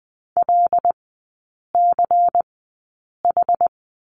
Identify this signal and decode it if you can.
Morse code, heard as a keyed tone, about 20 wpm, spaced as for 8 wpm, 700 Hz. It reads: LCH
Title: 152 Lesson 25e - Letter p - Three Characters 20wpm